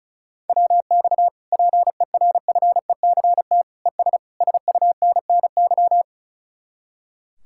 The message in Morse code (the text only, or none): WX PERFECT ES SUNNY